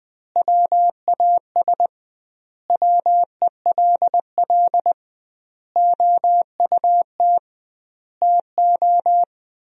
Morse code, heard as a keyed tone, about 21 wpm, 700 Hz. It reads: WAS WELL OUT TO
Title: 2x Word-Spacing - Sentences from Top 400 Words - 20wpm